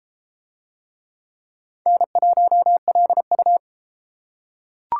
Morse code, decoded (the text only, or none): N1LU